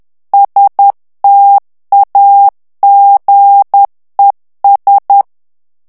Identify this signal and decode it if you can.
Morse code, heard as a keyed tone, about 10 wpm, 800 Hz. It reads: STAGES
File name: ANGLAIS texte clair 3 vitesse 10 mots minute